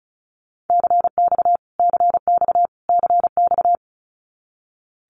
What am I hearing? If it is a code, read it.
Morse code, heard as a keyed tone, about 34 wpm, 700 Hz. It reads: CX CX CX
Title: Abbreviations Repeated 3x - 35wpm